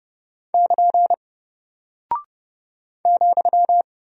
Morse code, heard as a keyed tone, about 30 wpm, 700 Hz. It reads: <KN> ,
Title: Single Character (Letter:Number:Punctuation:Prosign) 30wpm